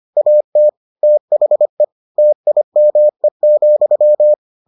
Morse code, heard as a keyed tone, about 25 wpm, 600 Hz. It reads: AT THE TIME,